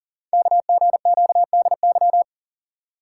Morse code, heard as a keyed tone, about 37 wpm, 700 Hz. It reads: KGQDY
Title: Code Groups - 5 Letters - 40wpm